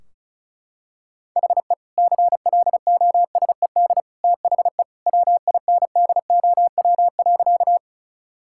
Morse code, heard as a keyed tone, about 36 wpm, 700 Hz. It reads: HE CLOSED THE WINDOW.
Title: Sentences from Top 1600 Words 35wpm